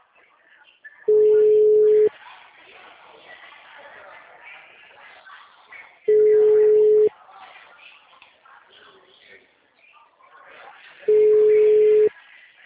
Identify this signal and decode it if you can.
Ringback tone (Europe, 425 Hz), with speech.